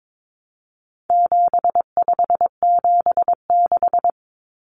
Morse code, heard as a keyed tone, about 22 wpm, 700 Hz. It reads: 7576
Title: Common Cut Numbers - 4-Digit 22wpm